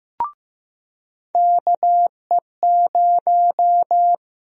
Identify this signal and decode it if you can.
Morse code, heard as a keyed tone, about 15 wpm, 700 Hz. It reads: KE0
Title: USA Call Sign Prefixes - Repeated 2x - 15wpm